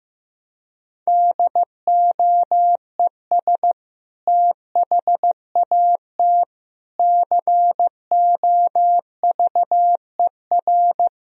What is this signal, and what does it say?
Morse code, heard as a keyed tone, about 15 wpm, 700 Hz. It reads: DOES THAT COVER
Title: Sentences from Top 1000 Words - Review Part 1 - 15wpm